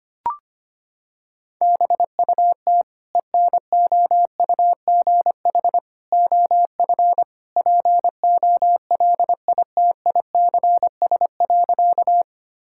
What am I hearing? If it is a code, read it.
Morse code, heard as a keyed tone, about 25 wpm, 700 Hz. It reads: BUT ENOUGH OF POLITICS.
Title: Sentences from Top 1000 Words - Review Part 6 - 25wpm